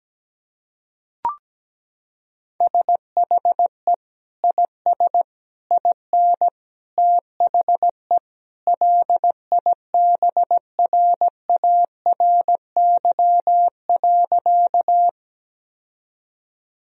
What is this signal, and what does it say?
Morse code, heard as a keyed tone, about 17 wpm, 700 Hz. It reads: SHE IS IN THE LIBRARY.
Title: Sentences from Top 1000 Words - Review Part 4 - 17wpm